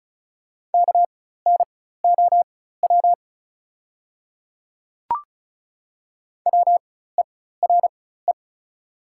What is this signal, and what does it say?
Morse code, heard as a keyed tone, about 34 wpm, spaced as for 16 wpm, 690 Hz. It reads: KNOW WERE
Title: 4x Character Spacing - Top 500 Words - 35wpm